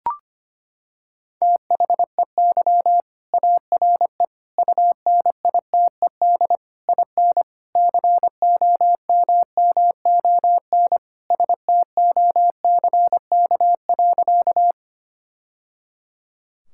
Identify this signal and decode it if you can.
Morse code, heard as a keyed tone, about 25 wpm, 700 Hz. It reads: THEY ARE UNITED IN COMMON STOCK.
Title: Sentences from Top 1000 Words - Review Part 5 - 25wpm